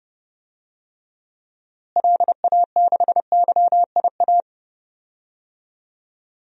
Morse code, heard as a keyed tone, about 30 wpm, 700 Hz. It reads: LA6YIA
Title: DXCC-100 Call Signs - 6 Characters - Repeated 3x - 30wpm